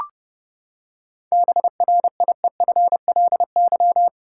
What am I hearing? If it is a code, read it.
Morse code, heard as a keyed tone, about 32 wpm, 700 Hz. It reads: BRIEFLY
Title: Seven Letter Words - 30wpm